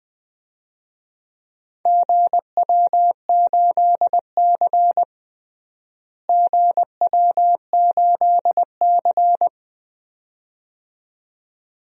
Morse code, heard as a keyed tone, about 20 wpm, 700 Hz. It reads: GW8C GW8C
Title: DXCC-100 Call Signs - 4 Characters - Repeated 2x - 20wpm